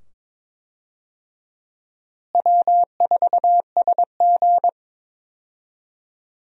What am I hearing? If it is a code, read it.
Morse code, heard as a keyed tone, about 22 wpm, 700 Hz. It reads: W4SG
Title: USA Call Signs - 4 Characters - Repeated 2x - 22wpm